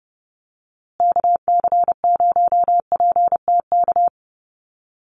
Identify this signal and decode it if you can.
Morse code, heard as a keyed tone, about 30 wpm, 700 Hz. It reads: KC0PTK
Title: USA Call Signs Repeated 2x 30wpm